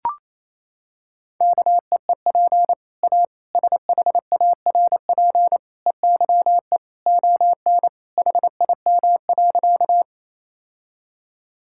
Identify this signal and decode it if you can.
Morse code, heard as a keyed tone, about 28 wpm, 700 Hz. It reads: KEEP A SHARP EYE ON HIM.
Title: Sentences from Top 1600 Words 28wpm